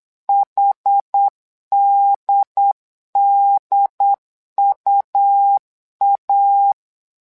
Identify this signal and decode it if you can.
Morse code, heard as a keyed tone, about 8 wpm, 800 Hz. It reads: HDDUA
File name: LEZIONI / 2 UHDN / 2 UHD 8wpm 2258